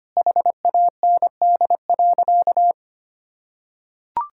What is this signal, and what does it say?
Morse code, heard as a keyed tone, about 25 wpm, 700 Hz. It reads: HAND.
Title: Sentences from Top 1000 Words - Review Part 5 - 25wpm